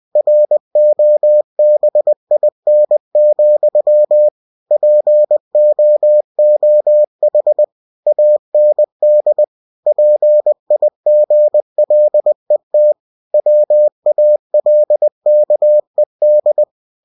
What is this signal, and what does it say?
Morse code, heard as a keyed tone, about 20 wpm, 600 Hz. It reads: ROBIN, POOH AND PIGLET WALKED
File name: cw-books / winnie-the-pooh / winnie-20wpm-page-0110